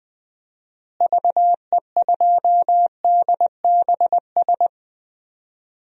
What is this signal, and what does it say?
Morse code, heard as a keyed tone, about 20 wpm, 700 Hz. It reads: VE2DBS
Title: DXCC-100 Call Signs - 6 Characters - Repeated 2x - 20wpm